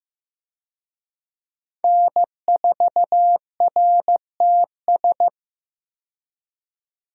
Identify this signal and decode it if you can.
Morse code, heard as a keyed tone, about 15 wpm, 700 Hz. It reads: N4RTS
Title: USA Call Signs - 5 Characters - 15wpm